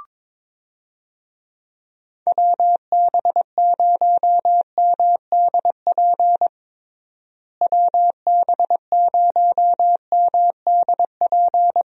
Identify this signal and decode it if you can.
Morse code, heard as a keyed tone, about 22 wpm, 700 Hz. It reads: WB0MDP WB0MDP
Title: USA Call Signs - 6 Characters - Repeated 3x - 22wpm